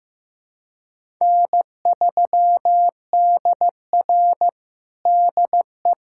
Morse code, heard as a keyed tone, about 15 wpm, 700 Hz. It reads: N3DR DE
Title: USA Callsign DE Callsign 15wpm